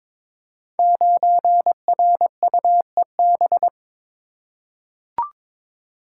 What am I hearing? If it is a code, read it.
Morse code, heard as a keyed tone, about 22 wpm, 700 Hz. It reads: 9RUEB
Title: Code Groups - 5 Letter:Numbers - 22wpm